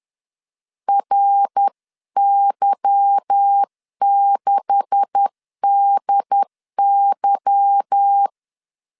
Morse code, heard as a keyed tone, about 10 wpm, 800 Hz. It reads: RY6DY
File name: LEZIONI / 9 286 10wpm 2735